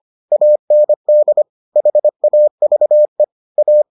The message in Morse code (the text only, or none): AND HAVE A